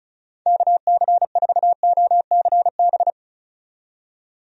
Morse code, heard as a keyed tone, about 36 wpm, 700 Hz. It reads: KC4OCB
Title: USA Call Signs - 6 Characters - 35wpm